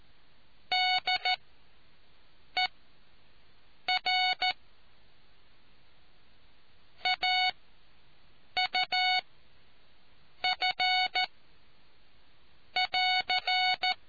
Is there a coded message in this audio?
Yes: Morse code from 0.7 to 13.9 s, heard as a keyed tone, about 14 wpm, spaced as for 6 wpm, 740 Hz. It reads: DER AUF<AR>